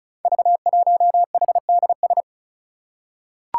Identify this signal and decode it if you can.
Morse code, heard as a keyed tone, about 36 wpm, 700 Hz. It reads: V1HDS
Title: Five Challenging Letter-Number Combinations 35wpm